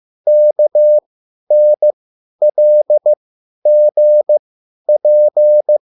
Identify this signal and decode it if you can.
Morse code, heard as a keyed tone, about 15 wpm, spaced as for 10 wpm, 600 Hz. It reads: KNLGP
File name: QSL / CW lesson